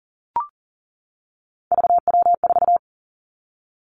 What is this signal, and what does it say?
Morse code, heard as a keyed tone, about 42 wpm, 690 Hz. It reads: VW4
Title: Three Challenging Letter-Number Combinations 40wpm